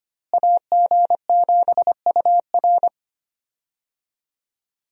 Morse code, heard as a keyed tone, about 25 wpm, 700 Hz. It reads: AG7UR
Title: USA Call Signs - 5 Characters - Repeated 2x - 25wpm